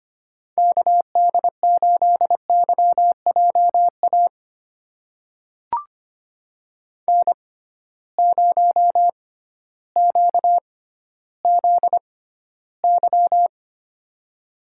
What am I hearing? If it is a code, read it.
Morse code, heard as a keyed tone, about 25 wpm, spaced as for 14 wpm, 700 Hz. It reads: KD8YJA N0QZY
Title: USA Call Signs - 6x Character Spacing - 25wpm